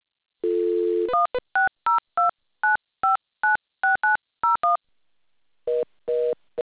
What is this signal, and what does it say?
DTMF, heard as telephone touch tones, tones of about 150 ms. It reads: 16029596901